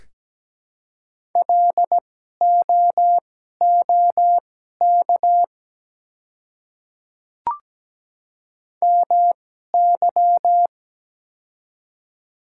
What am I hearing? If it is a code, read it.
Morse code, heard as a keyed tone, about 17 wpm, spaced as for 12 wpm, 690 Hz. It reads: LOOK MY
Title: 2x Character Spacing - Top 100 Words - 17wpm